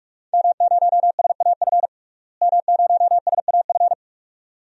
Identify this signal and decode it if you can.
Morse code, heard as a keyed tone, about 43 wpm, 700 Hz. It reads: M0SAF M0SAF
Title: DXCC-100 Call Signs - 5 Characters - Repeated 2x - 45wpm